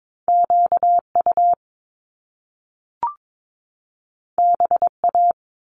Morse code, heard as a keyed tone, about 22 wpm, 700 Hz. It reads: QU BA
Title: Top Two Letter Combinations in Words 22wpm